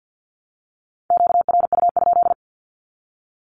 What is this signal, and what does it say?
Morse code, heard as a keyed tone, about 54 wpm, 700 Hz. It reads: QRU?
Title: Q-Codes Repeated 3x 50wpm